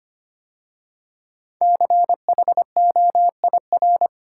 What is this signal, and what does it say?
Morse code, heard as a keyed tone, about 25 wpm, 700 Hz. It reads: CHOIR
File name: 5 Letter Words - First 3 Letters Given - 25wpm